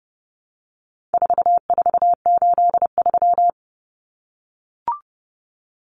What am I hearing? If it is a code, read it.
Morse code, heard as a keyed tone, about 30 wpm, 700 Hz. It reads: 4483
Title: Common Cut Numbers - 4-Digit 30wpm